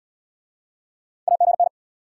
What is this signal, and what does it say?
Morse code, heard as a keyed tone, about 78 wpm, 700 Hz. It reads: AGN